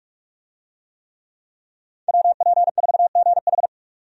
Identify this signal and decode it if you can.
Morse code, heard as a keyed tone, about 41 wpm, 700 Hz. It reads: WP4GH
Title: USA Call Signs - 5 Characters - 45wpm